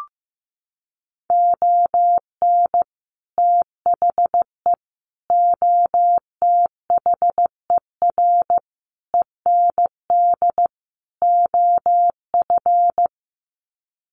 Morse code, heard as a keyed tone, about 14 wpm, 700 Hz. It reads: ON THE OTHER END OF
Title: Sets of 5 Words Encore 15wpm